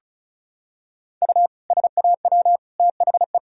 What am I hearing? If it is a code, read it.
Morse code, heard as a keyed tone, about 34 wpm, 700 Hz. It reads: U SAW THE
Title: Sentences from Top 400 Words and Abbreviations 35wpm